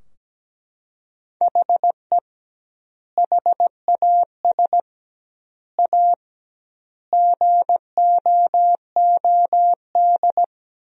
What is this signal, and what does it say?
Morse code, heard as a keyed tone, about 17 wpm, 700 Hz. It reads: HE HAS A GOOD
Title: 2x Word-Spacing - Sentences from Top 500 Words - 17wpm